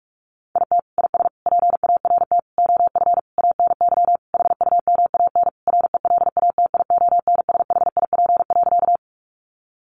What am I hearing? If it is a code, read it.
Morse code, heard as a keyed tone, about 44 wpm, 700 Hz. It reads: IT IS PART OF ANY HUMAN RELATIONSHIP.
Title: Sentences from Top 1100 Words 45wpm